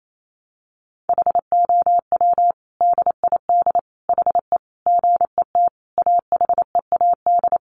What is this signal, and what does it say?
Morse code, heard as a keyed tone, about 28 wpm, 700 Hz. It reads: HOW DID HE GET AHEAD
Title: Sentences from Top 1100 Words 28wpm